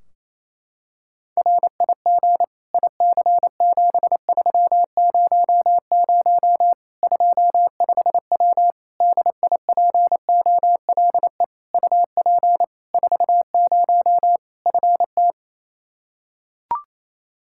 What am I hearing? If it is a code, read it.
Morse code, heard as a keyed tone, about 28 wpm, 700 Hz. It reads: RIG IC7300 25W DIPOLE UP 40 FT
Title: K1USN SST QSO Practice 28wpm